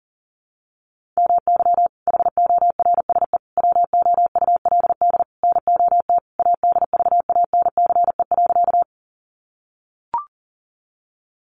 Morse code, heard as a keyed tone, about 40 wpm, 700 Hz. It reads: MY HORSE WOULD NOT ADVANCE.
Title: Sentences from Top 1400 Words 40wpm